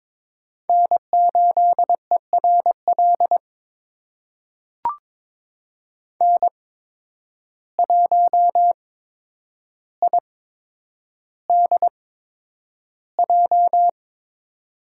Morse code, heard as a keyed tone, about 22 wpm, spaced as for 9 wpm, 700 Hz. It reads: N8ERL N1IDJ
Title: USA Call Signs - 8x Character Spacing - 22wpm